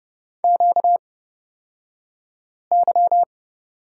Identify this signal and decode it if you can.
Morse code, heard as a keyed tone, about 30 wpm, spaced as for 5 wpm, 700 Hz. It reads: QY
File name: course / 30:10wpm / 235 Lesson 38d - Letter q - Two Characters 30wpm